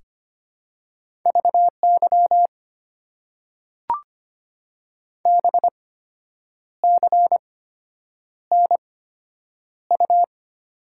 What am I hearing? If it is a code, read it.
Morse code, heard as a keyed tone, about 25 wpm, spaced as for 8 wpm, 690 Hz. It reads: VY BCNU
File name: Top 90 QSO Elements - 8x Character Spacing - 25wpm